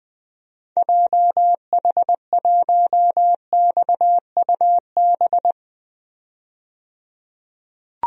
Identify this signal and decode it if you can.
Morse code, heard as a keyed tone, about 21 wpm, 700 Hz. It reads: JH1XUB
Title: DXCC-100 Call Signs - 6 Characters - 20wpm